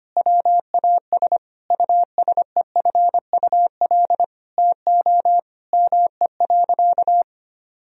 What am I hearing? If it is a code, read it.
Morse code, heard as a keyed tone, about 25 wpm, 700 Hz. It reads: WAS USEFUL TO ME.
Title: Sentences from Top 1000 Words - Review Part 4 - 25wpm